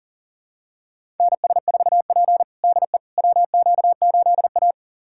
Morse code, heard as a keyed tone, about 40 wpm, 700 Hz. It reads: NS4P DE WQ8A
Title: USA Callsign DE Callsign - 4 Characters - 40wpm